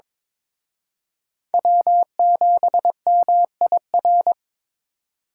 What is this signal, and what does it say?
Morse code, heard as a keyed tone, about 26 wpm, 700 Hz. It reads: W7MIR
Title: Speed Racing - USA 1x3 Call Signs 22wpm